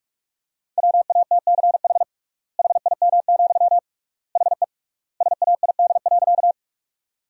Morse code, heard as a keyed tone, about 41 wpm, 700 Hz. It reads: WATCH HIM, HE SAID.
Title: 3x Word-Spacing - Sentences from Top 300 Words - 45wpm